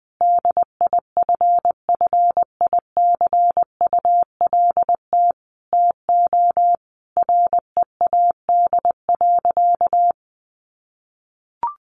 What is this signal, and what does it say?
Morse code, heard as a keyed tone, about 20 wpm, 700 Hz. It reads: DIFFICULT TO READ.
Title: Sentences from Top 1400 Words 20wpm